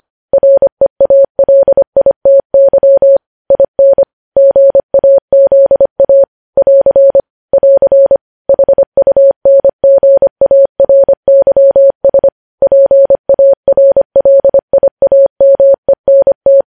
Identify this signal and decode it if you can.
Morse code, heard as a keyed tone, about 25 wpm, 550 Hz. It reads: REALITY IN GAZA <AR> <AR> HUNGARYS PARLIAMENT
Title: News Headlines at 25